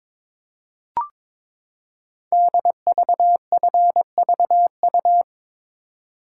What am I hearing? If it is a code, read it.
Morse code, heard as a keyed tone, about 22 wpm, 690 Hz. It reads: DVFVU
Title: Code Groups - 5 Letters - 22wpm